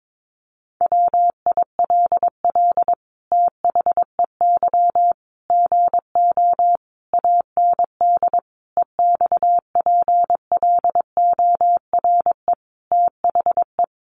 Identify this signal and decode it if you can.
Morse code, heard as a keyed tone, about 22 wpm, 700 Hz. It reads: WILL THEY GO AND EXPLORE THE
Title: Sentences from Top 1100 Words 22wpm